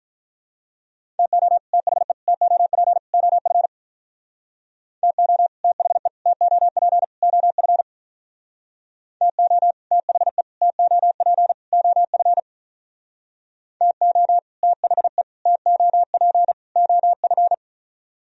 Morse code, heard as a keyed tone, about 45 wpm, 700 Hz. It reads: TO THE TOP OF TO THE TOP OF TO THE TOP OF TO THE TOP OF